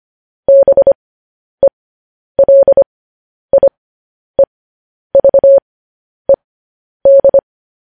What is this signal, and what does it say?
Morse code, heard as a keyed tone, about 25 wpm, spaced as for 10 wpm, 550 Hz. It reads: BELIEVED
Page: news-headlines-in-morse-code-at-10-wpm